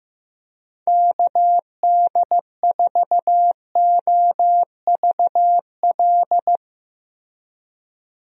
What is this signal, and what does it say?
Morse code, heard as a keyed tone, about 15 wpm, 700 Hz. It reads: KD4OVL
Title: USA Call Signs - 6 Characters - Repeated 3x - 15wpm